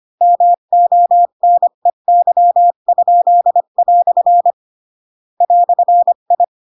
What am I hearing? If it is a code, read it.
Morse code, heard as a keyed tone, about 25 wpm, 700 Hz. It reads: MONEY?" "I
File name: Pinocchio / Pinocchio ch8 20 25 ik1hge